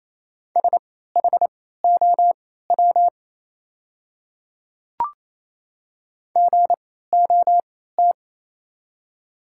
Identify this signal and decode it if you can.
Morse code, heard as a keyed tone, about 28 wpm, spaced as for 16 wpm, 700 Hz. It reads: SHOW GOT
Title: 3x Character Spacing - Top 500 Words - 28wpm